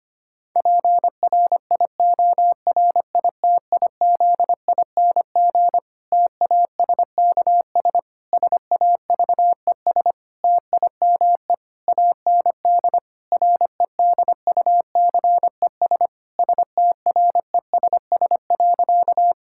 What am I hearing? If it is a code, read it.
Morse code, heard as a keyed tone, about 25 wpm, 700 Hz. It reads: PRIORITIZING TASKS SAVES TIME AND REDUCES STRESS.
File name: practice / 25wpm / Difficult Sentences with Hints - 25wpm